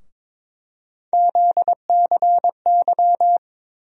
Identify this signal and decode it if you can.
Morse code, heard as a keyed tone, about 21 wpm, 700 Hz. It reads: ZCY
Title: USA Call Sign Postfixes - 3 Letters 22wpm